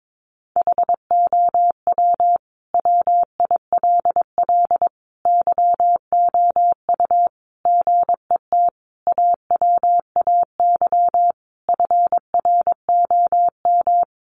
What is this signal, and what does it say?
Morse code, heard as a keyed tone, about 22 wpm, 700 Hz. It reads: HOW WILL YOU GET AWAY FROM
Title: Sentences from Top 1500 Words 22wpm